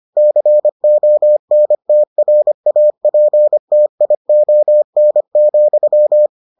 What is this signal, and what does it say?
Morse code, heard as a keyed tone, about 25 wpm, 600 Hz. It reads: CONTRAPTION,